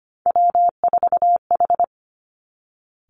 Morse code, heard as a keyed tone, about 25 wpm, 700 Hz. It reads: W4H
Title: Three Challenging Letter-Number Combinations 25wpm